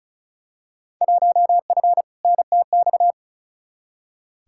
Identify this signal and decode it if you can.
Morse code, heard as a keyed tone, about 34 wpm, 700 Hz. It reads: 1F NTX
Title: Field Day Exchanges Encore - 35wpm